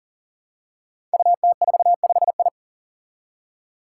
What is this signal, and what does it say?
Morse code, heard as a keyed tone, about 37 wpm, 700 Hz. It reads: UT45I
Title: Five Challenging Letter-Number Combinations 40wpm